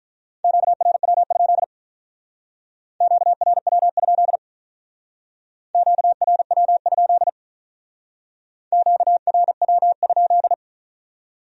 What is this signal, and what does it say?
Morse code, heard as a keyed tone, about 43 wpm, 700 Hz. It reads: QRW? QRW? QRW? QRW?
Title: Speed-Racing Q-Codes - 35wpm